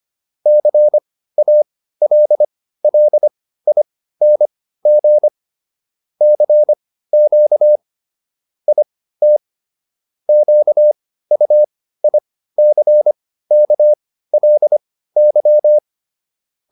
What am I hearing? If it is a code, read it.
Morse code, heard as a keyed tone, about 25 wpm, spaced as for 15 wpm, 600 Hz. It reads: CALLING CQ IT QUICKLY